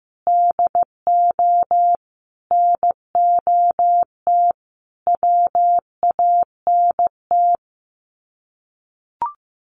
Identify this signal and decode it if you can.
Morse code, heard as a keyed tone, about 15 wpm, 700 Hz. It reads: DO NOT WANT
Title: Sets of 5 Words Encore2 15wpm